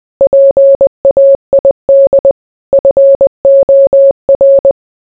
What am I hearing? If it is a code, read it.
Morse code, heard as a keyed tone, about 20 wpm, 550 Hz. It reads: PAID FOR